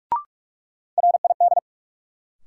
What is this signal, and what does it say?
Morse code, heard as a keyed tone, about 46 wpm, 700 Hz. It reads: RID
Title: Sets of 4 Words - New Word Drill - 45wpm